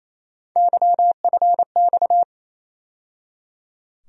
Morse code, heard as a keyed tone, about 28 wpm, 700 Hz. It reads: YFX
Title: USA Call Sign Postfixes - 3 Letters 28wpm